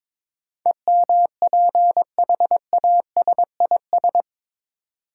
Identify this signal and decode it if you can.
Morse code, heard as a keyed tone, about 22 wpm, 700 Hz. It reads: EMPHASIS
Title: Sentences from Top 1400 Words - New Word Drill - 22wpm